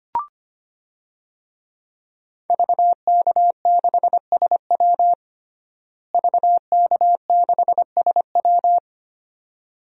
Morse code, heard as a keyed tone, about 25 wpm, 700 Hz. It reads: VK6SW VK6SW
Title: DXCC-100 Call Signs - 5 Characters - Repeated 2x - 25wpm